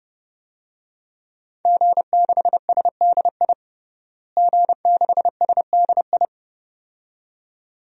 Morse code, heard as a keyed tone, about 30 wpm, 700 Hz. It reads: G6SDI G6SDI
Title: DXCC-100 Call Signs - 5 Characters - Repeated 2x - 30wpm